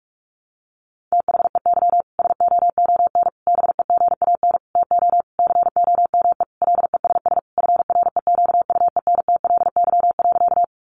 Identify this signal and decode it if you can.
Morse code, heard as a keyed tone, about 45 wpm, 700 Hz. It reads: THEY SOON BEGAN TO COME LESS FREQUENTLY.
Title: Sentences from Top 1300 Words 45wpm